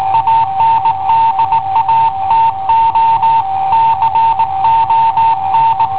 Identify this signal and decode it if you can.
Morse code, heard as a keyed tone, about 18 wpm, 910 Hz. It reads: ANDATO CON
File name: i7ohp / cassetta i7ohp B